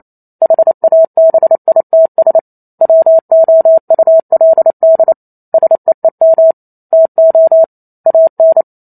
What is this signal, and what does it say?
Morse code, heard as a keyed tone, about 28 wpm, 650 Hz. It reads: HABITS WOULD SEEM TO AN